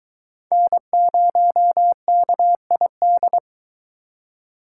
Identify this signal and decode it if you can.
Morse code, heard as a keyed tone, about 22 wpm, 700 Hz. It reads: N0KID